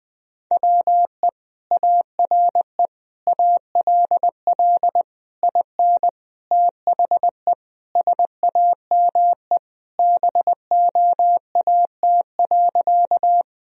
Morse code, heard as a keyed tone, about 21 wpm, 700 Hz. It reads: WE ARE ALL IN THE SAME BOAT.
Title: Sentences from Top 1000 Words - Review Part 3 - 20wpm